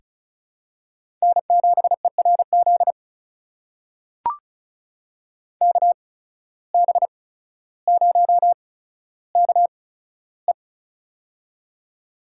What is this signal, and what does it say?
Morse code, heard as a keyed tone, about 35 wpm, spaced as for 15 wpm, 690 Hz. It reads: N7ERZ KB0KE
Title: USA Call Signs - 5 Characters - 8x Character Spacing - 35wpm